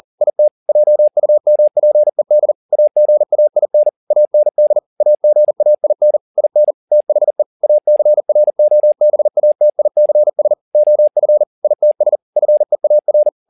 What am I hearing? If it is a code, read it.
Morse code, heard as a keyed tone, about 40 wpm, 600 Hz. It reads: IT JUMPED AGAIN AND AGAIN IN THE ACROBATICS OF ITS FEAR
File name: the old man and the sea 2 40wpm